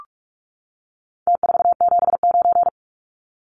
Morse code, heard as a keyed tone, about 44 wpm, 700 Hz. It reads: T479
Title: Numbers - Contest Style - 4-Digit 45wpm